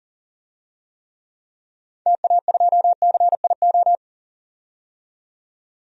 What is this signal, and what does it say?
Morse code, heard as a keyed tone, about 40 wpm, 700 Hz. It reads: TA2CIO
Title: DXCC-100 Call Signs - 6 Characters - 40wpm